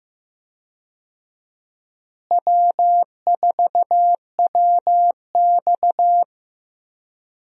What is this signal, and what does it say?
Morse code, heard as a keyed tone, about 15 wpm, 700 Hz. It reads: W4WX
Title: USA Call Signs - 4 Characters - Repeated 3x - 15wpm